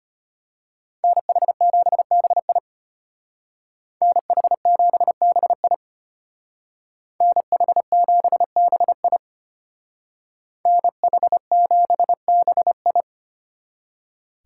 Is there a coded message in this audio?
Yes: Morse code from 1.0 to 13.0 s, heard as a keyed tone, about 33 wpm, 700 Hz. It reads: NH7BI NH7BI NH7BI NH7BI